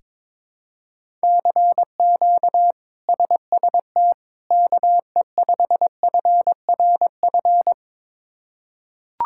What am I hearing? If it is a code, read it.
Morse code, heard as a keyed tone, about 22 wpm, 700 Hz. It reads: CQ SST KE5FRF
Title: K1USN SST QSO Practice - 22wpm